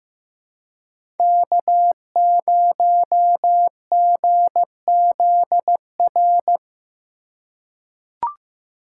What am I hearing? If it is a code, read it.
Morse code, heard as a keyed tone, about 15 wpm, 700 Hz. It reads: K0GZR
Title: USA Call Signs - 5 Characters - 3x Character Spacing - 15wpm